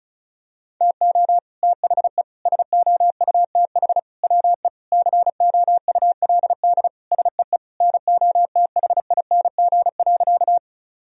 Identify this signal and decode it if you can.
Morse code, heard as a keyed tone, about 34 wpm, 700 Hz. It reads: TO THE SOUTH WE COULD SEE NOTHING.
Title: Sentences from Top 1000 Words - Review Part 2 - 35wpm